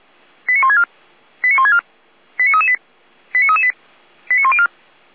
Five-tone selective call (ZVEI), heard as quick runs of five tones, tones of about 70 ms. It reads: 89164 89164 89338 89338 89104